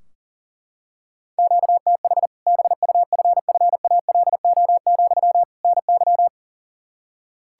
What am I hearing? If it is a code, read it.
Morse code, heard as a keyed tone, about 40 wpm, 700 Hz. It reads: QTH BUFFALO, NY
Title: Practice QSO - Exchange 1 - Beginning - 40wpm